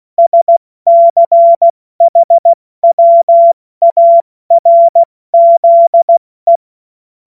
Morse code, heard as a keyed tone, about 16 wpm, 680 Hz. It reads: SCHWARZE
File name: Morsekurs / Witze / Witz 06. Januar 2026 70 F80